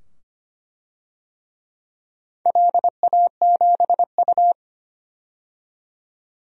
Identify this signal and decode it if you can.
Morse code, heard as a keyed tone, about 25 wpm, 700 Hz. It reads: LA7U
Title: DXCC-100 Call Signs - 4 Characters - 25wpm